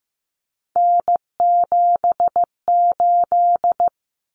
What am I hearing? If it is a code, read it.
Morse code, heard as a keyed tone, about 15 wpm, 700 Hz. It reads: N78
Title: Common Cut Numbers - 3-Digit 15wpm